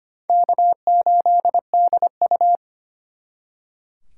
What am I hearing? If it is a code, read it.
Morse code, heard as a keyed tone, about 32 wpm, 700 Hz. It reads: K8DU